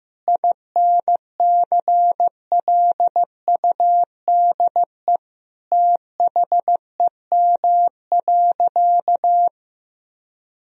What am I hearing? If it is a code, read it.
Morse code, heard as a keyed tone, about 15 wpm, 700 Hz. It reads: INCLUDE THEM.
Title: Sentences from Top 1000 Words - Review Part 3 - 15wpm